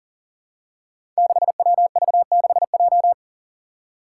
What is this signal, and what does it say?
Morse code, heard as a keyed tone, about 40 wpm, 700 Hz. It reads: 6WV6J
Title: Five Challenging Letter-Number Combinations 40wpm